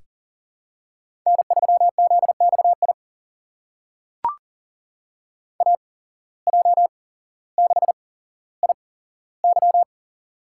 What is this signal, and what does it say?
Morse code, heard as a keyed tone, about 40 wpm, spaced as for 17 wpm, 700 Hz. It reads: N3ZXI AJ6IY
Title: USA Call Signs - 5 Characters - 8x Character Spacing - 40wpm